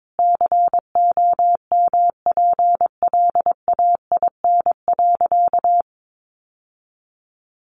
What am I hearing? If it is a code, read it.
Morse code, heard as a keyed tone, about 22 wpm, 700 Hz. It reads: COMPLAIN.
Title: Sentences from Top 1300 Words 22wpm